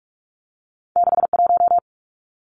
Morse code, heard as a keyed tone, about 46 wpm, 700 Hz. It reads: 61
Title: Numbers - 2-Digit 45wpm